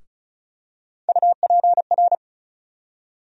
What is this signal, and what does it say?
Morse code, heard as a keyed tone, about 34 wpm, 700 Hz. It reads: UPR
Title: USA Call Sign Postfixes - 3 Letters 35wpm